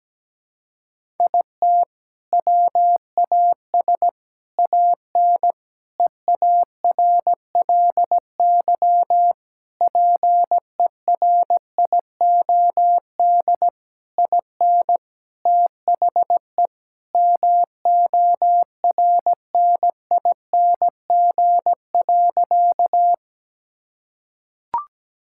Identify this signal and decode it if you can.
Morse code, heard as a keyed tone, about 17 wpm, 700 Hz. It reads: IT WAS AN EARLY PERIOD IN THE MORNING.
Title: Sentences from Top 1000 Words - Review Part 5 - 17wpm